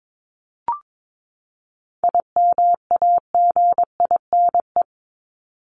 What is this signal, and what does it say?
Morse code, heard as a keyed tone, about 22 wpm, 700 Hz. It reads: IMAGINE
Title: Top 1251-1500 Words 22wpm